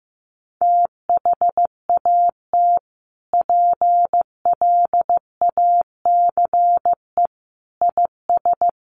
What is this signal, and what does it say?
Morse code, heard as a keyed tone, about 15 wpm, 700 Hz. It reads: THAT PLACE IS
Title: Sentences from Top 1500 Words 15wpm